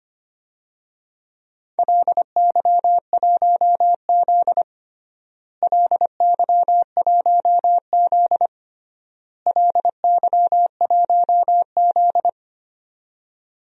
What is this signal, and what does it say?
Morse code, heard as a keyed tone, about 25 wpm, 700 Hz. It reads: LY1Z LY1Z LY1Z
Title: DXCC-100 Call Signs - 4 Characters - Repeated 3x - 25wpm